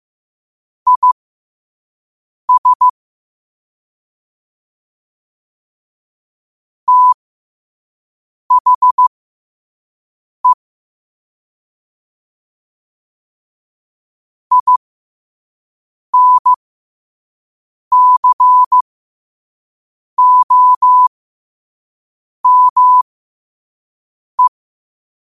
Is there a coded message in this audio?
Yes: Morse code from 0.9 to 24.5 s, heard as a keyed tone, about 15 wpm, spaced as for 5 wpm, 1000 Hz. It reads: IS THE INCOME